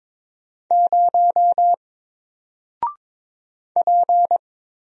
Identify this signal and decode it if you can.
Morse code, heard as a keyed tone, about 22 wpm, 700 Hz. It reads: 0 P
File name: Single-Letter-Number-22wpm